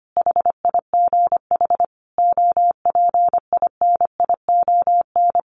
Morse code, heard as a keyed tone, about 25 wpm, 700 Hz. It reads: HIGH OPINION
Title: Sentences from Top 1300 Words 25wpm